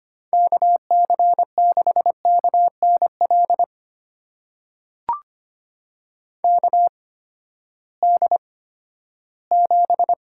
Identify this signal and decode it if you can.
Morse code, heard as a keyed tone, about 25 wpm, spaced as for 14 wpm, 700 Hz. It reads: KC6KNL KD7
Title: USA Call Signs - 6 Characters - 8x Character Spacing - 25wpm